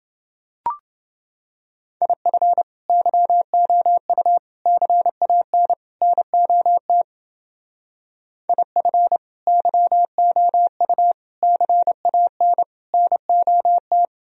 Morse code, heard as a keyed tone, about 29 wpm, 700 Hz. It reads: IF YOU CAN NOT IF YOU CAN NOT